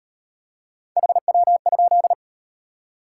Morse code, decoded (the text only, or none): HW?